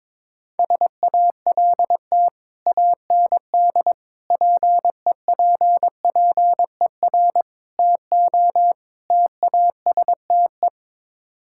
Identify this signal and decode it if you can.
Morse code, heard as a keyed tone, about 22 wpm, 700 Hz. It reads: SALT AND PEPPER TO TASTE
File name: practice / 22wpm / Build A Phrase - 5-Word Phrases - 22wpm